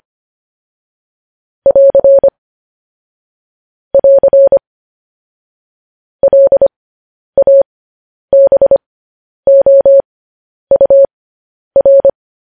Morse code, heard as a keyed tone, about 25 wpm, spaced as for 10 wpm, 550 Hz. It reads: <AR> <AR> LABOUR
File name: podcast / News headlines-1x-25x10